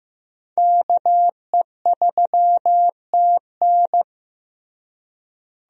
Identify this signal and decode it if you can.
Morse code, heard as a keyed tone, about 15 wpm, 700 Hz. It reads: KE3TN
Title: USA Call Signs - 5 Characters - Repeated 3x - 15wpm